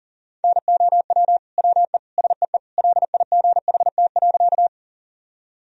Morse code, heard as a keyed tone, about 40 wpm, 700 Hz. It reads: NOW WE SEE LIGHT.
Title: Sentences from Top 1000 Words - Review Part 1 - 40wpm